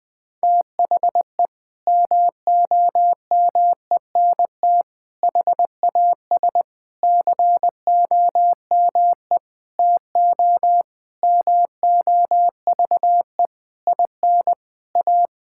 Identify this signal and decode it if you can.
Morse code, heard as a keyed tone, about 20 wpm, 700 Hz. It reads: THE MOMENT HAS COME TO MOVE IN A
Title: Sentences from Top 1000 Words - Review Part 5 - 20wpm